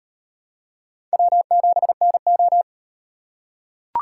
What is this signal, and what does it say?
Morse code, heard as a keyed tone, about 38 wpm, 700 Hz. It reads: W7NO